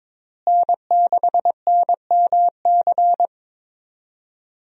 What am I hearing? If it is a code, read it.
Morse code, heard as a keyed tone, about 22 wpm, 700 Hz. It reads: N6NMC
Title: USA Call Signs - 5 Characters - Repeated 2x - 22wpm